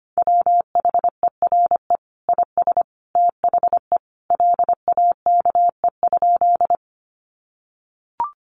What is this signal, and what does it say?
Morse code, heard as a keyed tone, about 25 wpm, 700 Hz. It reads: WHERE IS THE LAKE?
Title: Sentences from Top 1100 Words 25wpm